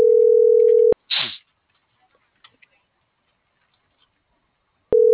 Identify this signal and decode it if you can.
Ringback tone (North America), 440 + 480 Hz.